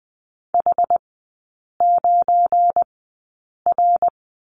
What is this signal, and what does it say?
Morse code, heard as a keyed tone, about 20 wpm, spaced as for 8 wpm, 700 Hz. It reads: H9R
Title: 172 Lesson 28e - Number 9 - Three Characters 20wpm